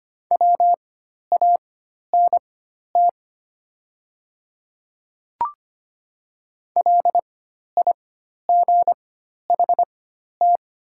Morse code, heard as a keyed tone, about 25 wpm, spaced as for 12 wpm, 690 Hz. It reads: WANT LIGHT
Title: 4x Character Spacing - Top 200 Words - 25wpm